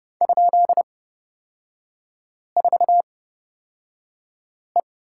Morse code, heard as a keyed tone, about 30 wpm, spaced as for 5 wpm, 700 Hz. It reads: ?4E
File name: Lesson 36e - BT - Three Characters 30wpm